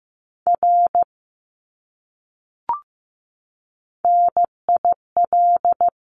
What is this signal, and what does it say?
Morse code, heard as a keyed tone, about 15 wpm, 700 Hz. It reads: R NIL
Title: Abbreviations Repeated 3x - 15wpm